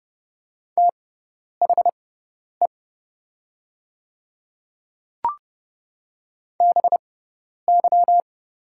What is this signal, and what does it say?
Morse code, heard as a keyed tone, about 30 wpm, spaced as for 10 wpm, 700 Hz. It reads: THE BY
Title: 6x Character Spacing - Top 100 Words - 30wpm